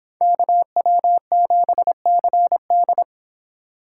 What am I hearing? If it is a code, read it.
Morse code, heard as a keyed tone, about 26 wpm, 700 Hz. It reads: KW7CD